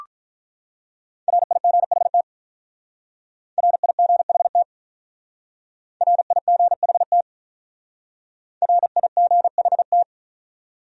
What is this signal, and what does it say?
Morse code, heard as a keyed tone, about 45 wpm, 700 Hz. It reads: RIGHT RIGHT RIGHT RIGHT